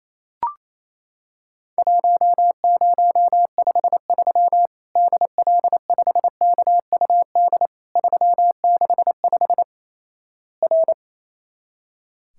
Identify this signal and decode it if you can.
Morse code, heard as a keyed tone, about 28 wpm, 700 Hz. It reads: 1053 DL5KUD 365 R
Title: Worked All Europe DX Contest - QTC Exchange - 28wpm